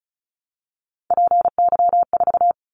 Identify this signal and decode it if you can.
Morse code, heard as a keyed tone, about 36 wpm, 700 Hz. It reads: PY4
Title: Callsign Prefixes for Top 100 DXCC - 35wpm